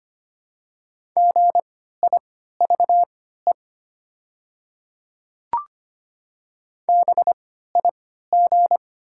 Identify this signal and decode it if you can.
Morse code, heard as a keyed tone, about 25 wpm, spaced as for 14 wpm, 700 Hz. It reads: GIVE BIG
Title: 3x Character Spacing - Top 200 Words - 25wpm